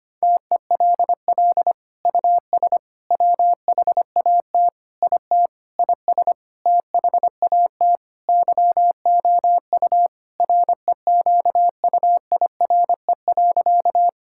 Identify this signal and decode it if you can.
Morse code, heard as a keyed tone, about 25 wpm, 700 Hz. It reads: TELL US WHAT IT IS THAT YOU REQUIRE.
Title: Sentences from Top 1000 Words - Review Part 6 - 25wpm